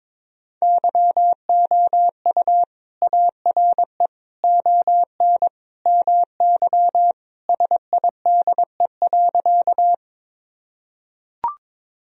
Morse code, heard as a keyed tone, about 22 wpm, 700 Hz. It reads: YOU ARE ON MY SIDE.
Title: Sentences from Top 1000 Words - Review Part 1 - 22wpm